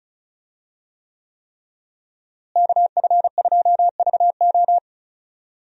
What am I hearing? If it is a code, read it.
Morse code, heard as a keyed tone, about 35 wpm, 700 Hz. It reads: KF2VO
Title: USA Call Signs - 5 Characters - Repeated 3x - 35wpm